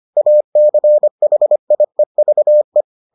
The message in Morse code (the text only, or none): ACHIEVE